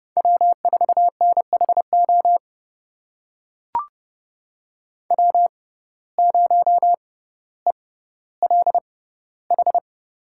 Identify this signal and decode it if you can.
Morse code, heard as a keyed tone, about 30 wpm, spaced as for 15 wpm, 700 Hz. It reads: W4NHO W0ELH
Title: USA Call Signs - 5 Characters - 6x Character Spacing - 30wpm